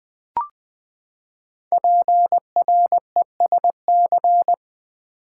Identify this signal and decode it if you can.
Morse code, heard as a keyed tone, about 20 wpm, 700 Hz. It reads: PRESC
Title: Build A Word - 5 Letters At A Time - 12-Letter Words - 20wpm